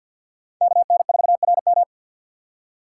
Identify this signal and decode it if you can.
Morse code, heard as a keyed tone, about 53 wpm, 700 Hz. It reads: KN4RM
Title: USA Call Signs - 3x Character Spacing - 50wpm